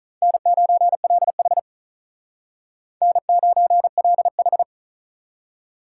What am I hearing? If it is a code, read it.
Morse code, heard as a keyed tone, about 45 wpm, 700 Hz. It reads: N9LH N9LH